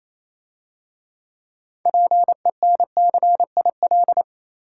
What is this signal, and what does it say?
Morse code, heard as a keyed tone, about 27 wpm, 690 Hz. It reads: PENCIL
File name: Easy Words - Speed-Racing with Hints - 22wpm